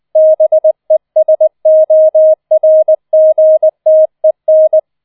Morse code, heard as a keyed tone, about 20 wpm, 630 Hz. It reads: BESORGTEN